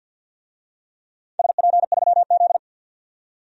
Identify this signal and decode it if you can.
Morse code, heard as a keyed tone, about 49 wpm, 700 Hz. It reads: SP3Z
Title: DXCC-100 Call Signs - 4 Characters - 50wpm